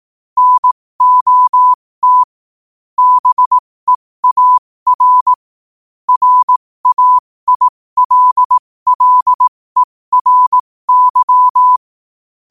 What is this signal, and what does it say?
Morse code, heard as a keyed tone, about 18 wpm, spaced as for 15 wpm, 1000 Hz. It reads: NOT BEAR RAILLERY
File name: qotdfiles / 15wpm